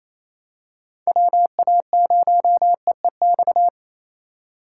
Morse code, heard as a keyed tone, about 28 wpm, 700 Hz. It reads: WA0EEX
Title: USA Call Signs - 6 Characters - 2x Character Spacing - 28wpm